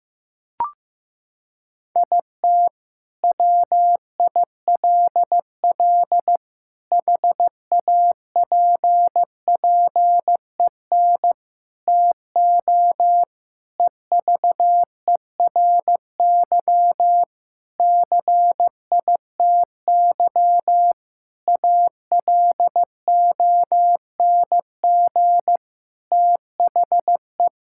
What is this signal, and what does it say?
Morse code, heard as a keyed tone, about 15 wpm, 700 Hz. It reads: IT WILL HAPPEN TO EVERY CITY ALONG THE
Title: Sentences from Top 1000 Words - Review Part 6 - 15wpm